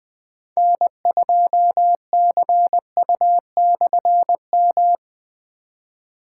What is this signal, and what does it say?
Morse code, heard as a keyed tone, about 20 wpm, 700 Hz. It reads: N2CU/M
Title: K1USN SST QSO Practice - 20wpm